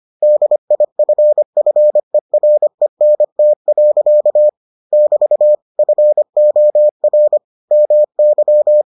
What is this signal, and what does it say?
Morse code, heard as a keyed tone, about 25 wpm, 600 Hz. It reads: DIFFERENT. <BT> FOR MY